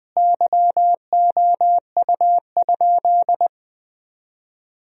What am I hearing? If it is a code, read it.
Morse code, heard as a keyed tone, about 20 wpm, 700 Hz. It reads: YOU?
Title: 3x Word-Spacing - Sentences from Top 300 Words - 20wpm